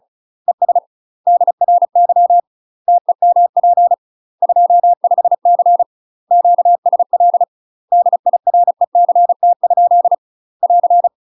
Morse code, heard as a keyed tone, about 34 wpm, 700 Hz. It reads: ES DRY TEMP 25C QSL DIRECT? <AR>
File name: UK MASTER QSOs TRAINING / 35WPM / UK MASTER QSOs 0037